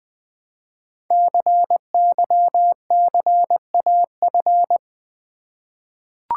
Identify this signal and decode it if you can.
Morse code, heard as a keyed tone, about 20 wpm, 700 Hz. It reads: CYCAF
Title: Code Groups - 5 Letters - 20wpm